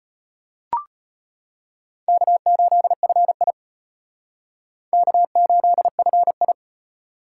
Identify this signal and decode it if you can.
Morse code, heard as a keyed tone, about 34 wpm, 700 Hz. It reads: K8FI K8FI